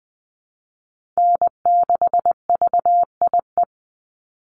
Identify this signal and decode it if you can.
Morse code, heard as a keyed tone, about 20 wpm, 700 Hz. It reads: N6VIE
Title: USA Call Signs 20wpm